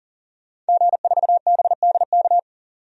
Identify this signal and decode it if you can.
Morse code, heard as a keyed tone, about 40 wpm, 700 Hz. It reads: G4BDK
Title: DXCC-100 Call Signs - 5 Characters - 40wpm